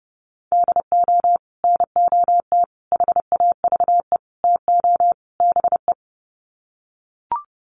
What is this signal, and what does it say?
Morse code, heard as a keyed tone, about 28 wpm, 700 Hz. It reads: DO NOT HAVE TO BE
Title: Sets of 5 Words 30wpm